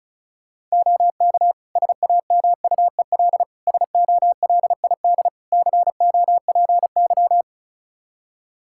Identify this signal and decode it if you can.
Morse code, heard as a keyed tone, about 34 wpm, 700 Hz. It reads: OK SAMUEL SOLID COPY
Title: K1USN SST QSO Practice 35wpm